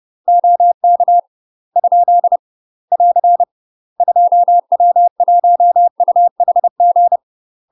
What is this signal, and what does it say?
Morse code, heard as a keyed tone, about 30 wpm, 700 Hz. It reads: OK ? <AR> 2W1UHG
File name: UK MASTER QSOs TRAINING/30WPM